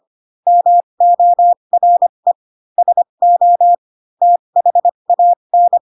Morse code, heard as a keyed tone, about 25 wpm, spaced as for 22 wpm, 700 Hz. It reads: MORE SO THAN
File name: Pinocchio / Pinocchio ch3 18 25 ik1hge